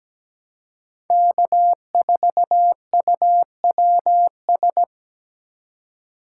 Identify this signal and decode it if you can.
Morse code, heard as a keyed tone, about 17 wpm, 700 Hz. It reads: K4UWS
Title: USA Call Signs - 5 Characters - 2x Character Spacing - 17wpm